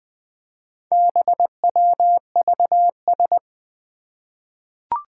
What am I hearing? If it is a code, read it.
Morse code, heard as a keyed tone, about 21 wpm, 700 Hz. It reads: BWVS
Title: Four Challenging Letter-Number Combinations 20wpm